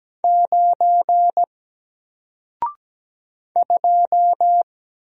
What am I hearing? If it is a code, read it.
Morse code, heard as a keyed tone, about 17 wpm, 700 Hz. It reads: 9 2
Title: Single Character (Letter:Number:Punctuation:Prosign) Alt 17wpm